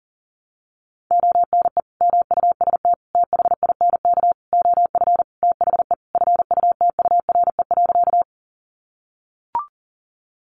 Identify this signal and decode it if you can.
Morse code, heard as a keyed tone, about 40 wpm, 700 Hz. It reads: ONE MUST THINK OF THE FUTURE.
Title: Sentences from Top 700 Words 40wpm